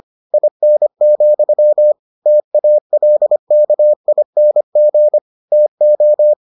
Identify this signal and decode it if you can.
Morse code, heard as a keyed tone, about 25 wpm, 600 Hz. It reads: IN, TALKING TO